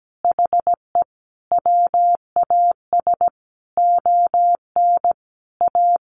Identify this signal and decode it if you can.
Morse code, heard as a keyed tone, about 17 wpm, 700 Hz. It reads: HE WAS ON A
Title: Sentences from Top 1100 Words 17wpm